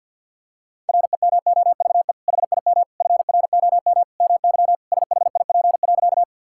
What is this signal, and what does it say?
Morse code, heard as a keyed tone, about 50 wpm, 700 Hz. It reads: REMOVE HIM FROM MY SHIP.
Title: Sentences from Top 1000 Words - Review Part 5 - 50wpm